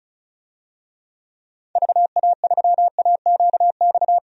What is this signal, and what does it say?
Morse code, heard as a keyed tone, about 36 wpm, 700 Hz. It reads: VA3AQX
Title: DXCC-100 Call Signs - 6 Characters - 35wpm